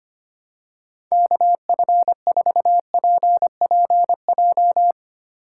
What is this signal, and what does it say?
Morse code, heard as a keyed tone, about 25 wpm, 700 Hz. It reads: KF4PPJ
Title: USA Call Signs - 6 Characters - 4x Character Spacing - 25wpm